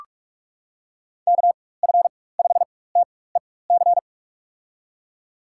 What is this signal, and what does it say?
Morse code, heard as a keyed tone, about 46 wpm, spaced as for 21 wpm, 700 Hz. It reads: KF5TEC
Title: USA Call Signs - 6 Characters - 4x Character Spacing - 45wpm